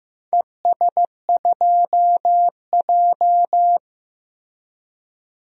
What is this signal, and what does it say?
Morse code, heard as a keyed tone, about 16 wpm, 700 Hz. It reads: ES2J
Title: DXCC-100 Call Signs - 4 Characters - Repeated 2x - 15wpm